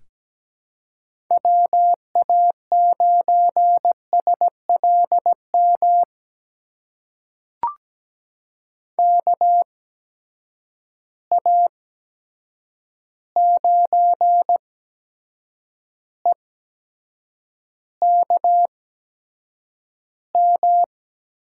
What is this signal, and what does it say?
Morse code, heard as a keyed tone, about 17 wpm, spaced as for 7 wpm, 700 Hz. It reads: WA9SLM KA9EKM
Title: USA Call Signs - 6 Characters - 8x Character Spacing - 17wpm